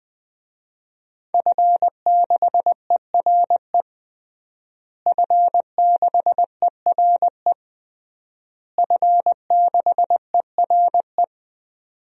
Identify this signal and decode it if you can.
Morse code, heard as a keyed tone, about 20 wpm, 700 Hz. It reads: F6ERE F6ERE F6ERE
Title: DXCC-100 Call Signs - 5 Characters - Repeated 3x - 20wpm